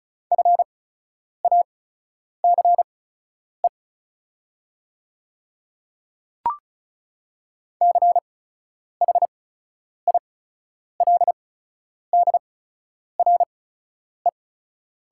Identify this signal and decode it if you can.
Morse code, heard as a keyed tone, about 36 wpm, spaced as for 10 wpm, 690 Hz. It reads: FACE CHILDRE
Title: 8x Character Spacing - Top 300 Words - 35wpm